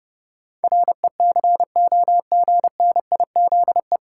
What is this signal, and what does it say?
Morse code, heard as a keyed tone, about 30 wpm, 700 Hz. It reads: RECOGNIZE
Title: Sentences from Top 1400 Words - New Word Drill Alt - 30wpm